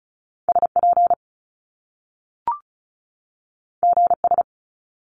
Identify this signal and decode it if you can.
Morse code, heard as a keyed tone, about 36 wpm, 700 Hz. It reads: SP GS
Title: Top Two Letter Combinations in Words 35wpm